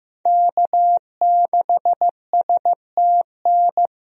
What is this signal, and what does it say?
Morse code, heard as a keyed tone, about 15 wpm, 700 Hz. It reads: K6STN
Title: USA Call Signs - 5 Characters - Repeated 2x - 15wpm